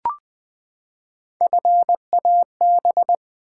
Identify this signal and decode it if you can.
Morse code, heard as a keyed tone, about 20 wpm, 690 Hz. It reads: FAB
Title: Build A Word - 3 Letters At A Time - 8-Letter Words - 20wpm